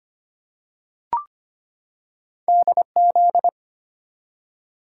Morse code, heard as a keyed tone, about 25 wpm, 700 Hz. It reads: DZ
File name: Two Letter-Number Combinations - 25wpm